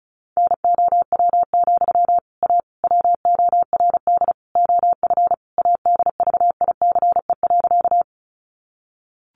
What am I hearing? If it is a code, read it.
Morse code, heard as a keyed tone, about 34 wpm, 700 Hz. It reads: NOW, A WORD OF ADVICE.
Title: Sentences from Top 1100 Words 35wpm